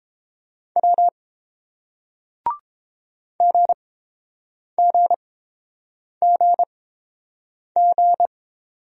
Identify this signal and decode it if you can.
Morse code, heard as a keyed tone, about 30 wpm, spaced as for 7 wpm, 700 Hz. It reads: W GGGG